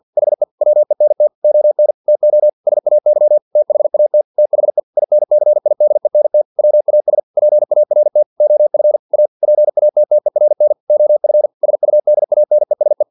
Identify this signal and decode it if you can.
Morse code, heard as a keyed tone, about 50 wpm, 600 Hz. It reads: HE WENT ON TO SAY THAT THE INCIDENT WAS PART OF A PATTERN OF SUDANESE
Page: Never 49wpm